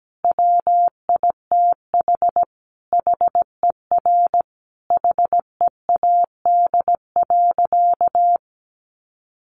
Morse code, heard as a keyed tone, about 17 wpm, 700 Hz. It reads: WITH HER HEAD.
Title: Sentences from Top 300 Words 17wpm